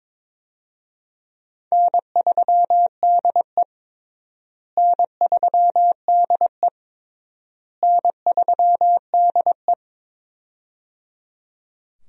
Morse code, heard as a keyed tone, about 22 wpm, 700 Hz. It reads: N3DE N3DE N3DE
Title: USA Call Signs - 4 Characters - Repeated 3x - 22wpm